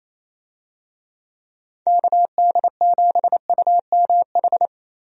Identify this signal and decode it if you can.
Morse code, heard as a keyed tone, about 28 wpm, 700 Hz. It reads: KD7UMH
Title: USA Call Signs - 6 Characters - 28wpm